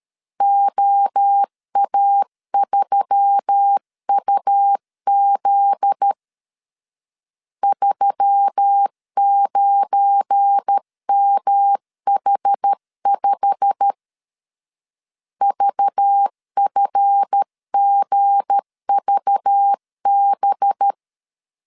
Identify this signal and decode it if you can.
Morse code, heard as a keyed tone, about 13 wpm, 800 Hz. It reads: OA3UZ 39MH5 VFGVB
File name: LEZIONI / 8 YJ45 / 8 YJ45 12wpm 2151